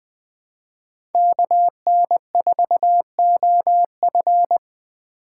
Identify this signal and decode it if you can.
Morse code, heard as a keyed tone, about 20 wpm, 700 Hz. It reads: KN4OF
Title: USA Call Signs - 5 Characters - 3x Character Spacing - 20wpm